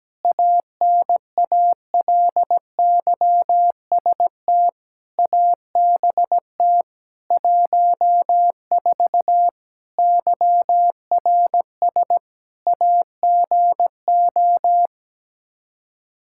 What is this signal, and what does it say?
Morse code, heard as a keyed tone, about 17 wpm, 700 Hz. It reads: ANALYST ABT 14 YRS AGO
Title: Practice QSO - Exchange 3 - More Info - 17wpm